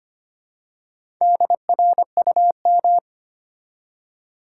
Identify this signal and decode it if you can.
Morse code, heard as a keyed tone, about 25 wpm, 700 Hz. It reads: DRUM
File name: practice / 25wpm / Easy Words with Hints - 25wpm